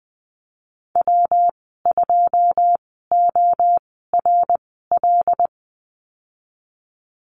Morse code, heard as a keyed tone, about 19 wpm, spaced as for 14 wpm, 700 Hz. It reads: W2ORL
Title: 147 Lesson 24g - Letter y - Call Signs 20wpm